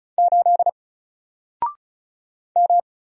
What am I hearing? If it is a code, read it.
Morse code, heard as a keyed tone, about 36 wpm, 700 Hz. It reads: 8 M
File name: Single-Letter-Number-35wpm